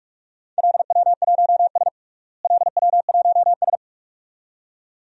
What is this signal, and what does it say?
Morse code, heard as a keyed tone, about 46 wpm, 700 Hz. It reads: LW1S LW1S
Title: DXCC-100 Call Signs - 4 Characters - Repeated 2x - 45wpm